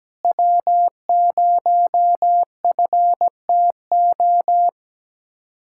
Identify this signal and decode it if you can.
Morse code, heard as a keyed tone, about 17 wpm, 700 Hz. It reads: W0FTO
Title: USA Callsign DE Callsign - 5 Characters - 17wpm